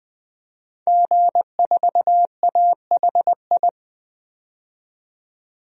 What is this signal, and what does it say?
Morse code, heard as a keyed tone, about 20 wpm, 700 Hz. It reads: G4AHI
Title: DXCC-100 Call Signs - 5 Characters - 20wpm